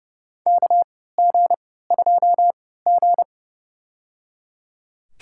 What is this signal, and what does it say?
Morse code, heard as a keyed tone, about 30 wpm, spaced as for 17 wpm, 700 Hz. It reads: KG2G
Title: USA Call Signs - 4 Characters - 3x Character Spacing - 30wpm